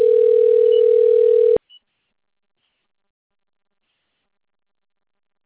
Ringback tone (North America), 440 + 480 Hz.
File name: out- - - - - .65660